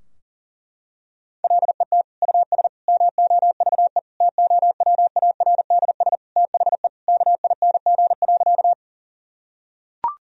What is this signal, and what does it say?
Morse code, heard as a keyed tone, about 40 wpm, 700 Hz. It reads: LET US MOVE TOWARDS THE KING.
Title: Sentences from Top 1000 Words - Review Part 2 - 40wpm